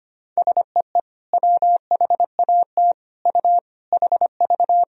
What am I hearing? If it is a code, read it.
Morse code, heard as a keyed tone, about 25 wpm, 700 Hz. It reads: SEE WHAT U HV
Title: Sentences from Top 500 Words and Abbreviations 25wpm